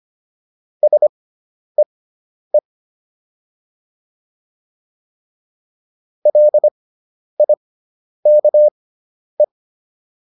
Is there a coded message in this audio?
Yes: Morse code from 0.8 to 9.4 s, heard as a keyed tone, about 25 wpm, spaced as for 10 wpm, 600 Hz. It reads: SEE LIKE